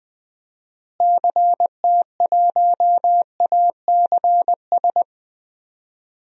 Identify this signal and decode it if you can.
Morse code, heard as a keyed tone, about 20 wpm, 700 Hz. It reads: CT1ACS
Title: DXCC-100 Call Signs - 6 Characters - Repeated 2x - 20wpm